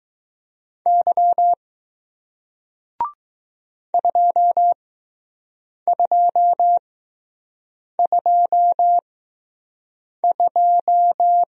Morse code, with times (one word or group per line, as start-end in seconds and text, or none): 0.9-1.5 Y
3.9-11.4 2222